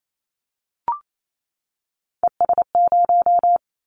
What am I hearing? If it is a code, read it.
Morse code, heard as a keyed tone, about 28 wpm, 700 Hz. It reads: ES0
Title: Callsign Prefixes for Top 100 DXCC - 28wpm